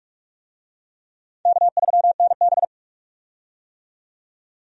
Morse code, heard as a keyed tone, about 48 wpm, 700 Hz. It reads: K3NB
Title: USA Call Signs - 4 Characters - 45wpm